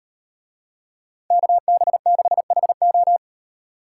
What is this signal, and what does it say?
Morse code, heard as a keyed tone, about 36 wpm, 700 Hz. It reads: KB6HO